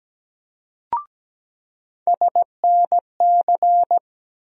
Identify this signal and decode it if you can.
Morse code, heard as a keyed tone, about 17 wpm, 700 Hz. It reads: SNC
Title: USA Call Sign Postfixes - 3 Letters 17wpm